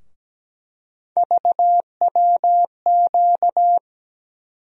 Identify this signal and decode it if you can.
Morse code, heard as a keyed tone, about 17 wpm, 700 Hz. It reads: VWQ
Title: USA Call Sign Postfixes - 3 Letters 17wpm